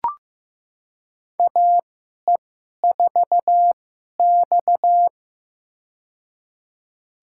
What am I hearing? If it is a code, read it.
Morse code, heard as a keyed tone, about 15 wpm, spaced as for 11 wpm, 700 Hz. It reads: AE4X
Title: USA Call Signs - 4 Characters - 2x Character Spacing - 15wpm